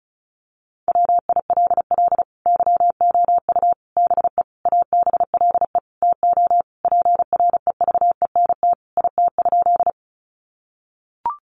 Morse code, heard as a keyed tone, about 34 wpm, 700 Hz. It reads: WILL YOU BE ABLE TO PREVENT IT?
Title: Sentences from Top 1100 Words 35wpm